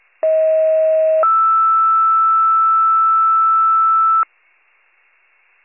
Two-tone page, heard as a one-second tone, then a three-second tone, about 640 then 1320 Hz. It is heard from 0.2 to 4.2 s.